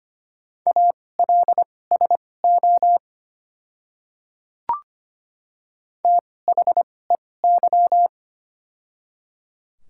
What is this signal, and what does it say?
Morse code, heard as a keyed tone, about 25 wpm, spaced as for 18 wpm, 700 Hz. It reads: ALSO THEY
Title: 2x Character Spacing - Top 300 Words - 25wpm